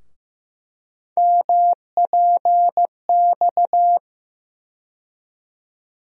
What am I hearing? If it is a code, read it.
Morse code, heard as a keyed tone, about 15 wpm, 700 Hz. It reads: MPX
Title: USA Call Sign Postfixes - 3 Letters 15wpm